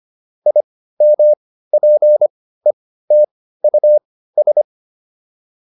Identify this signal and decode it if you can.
Morse code, heard as a keyed tone, about 25 wpm, spaced as for 15 wpm, 600 Hz. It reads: IMPETUS